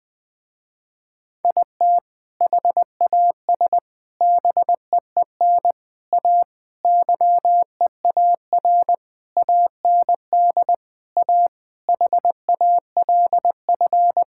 Morse code, heard as a keyed tone, about 21 wpm, 700 Hz. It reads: IT HAS BEEN A YEAR AND A HALF
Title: Sentences with 4-Word Phrases - Spoken Beginning - 20wpm